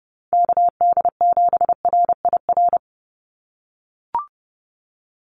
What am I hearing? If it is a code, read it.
Morse code, heard as a keyed tone, about 30 wpm, 700 Hz. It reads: KD7RIR
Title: USA Call Signs 30wpm